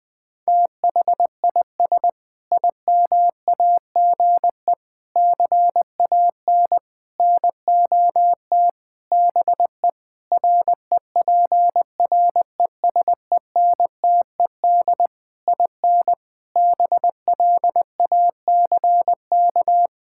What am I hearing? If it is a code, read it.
Morse code, heard as a keyed tone, about 20 wpm, 700 Hz. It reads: THIS IMAGE CAN NOT BE REPRESENTED IN BLACK
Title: Sentences with 5-Word Phrases - Spoken Beginning - 20wpm